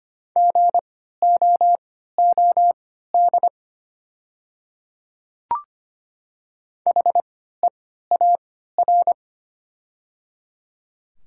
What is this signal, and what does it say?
Morse code, heard as a keyed tone, about 25 wpm, spaced as for 14 wpm, 700 Hz. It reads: GOOD HEAR
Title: 3x Character Spacing - Top 300 Words - 25wpm